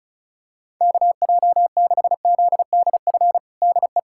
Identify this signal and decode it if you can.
Morse code, heard as a keyed tone, about 34 wpm, 700 Hz. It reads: KJ6ZDF DE
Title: USA Callsign DE Callsign - 6 Characters - 35wpm